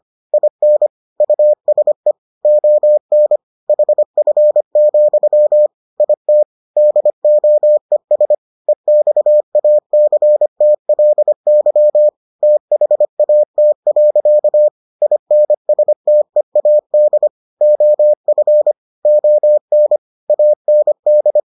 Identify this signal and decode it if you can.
Morse code, heard as a keyed tone, about 25 wpm, 600 Hz. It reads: IN USE ON HF, IT DOES EXACTLY THAT. INSTEAD OF ON AND